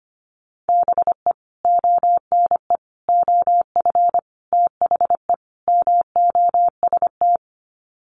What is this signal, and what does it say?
Morse code, heard as a keyed tone, about 25 wpm, 700 Hz. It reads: BE ONE OF THE MOST
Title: Sets of 5 Words Encore 25wpm